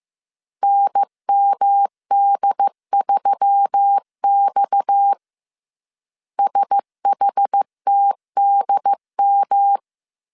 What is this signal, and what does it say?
Morse code, heard as a keyed tone, about 15 wpm, 800 Hz. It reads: NMD3X SHTDM